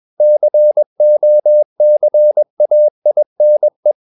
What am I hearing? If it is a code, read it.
Morse code, heard as a keyed tone, about 21 wpm, 600 Hz. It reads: COCAINE
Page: CW med Gnister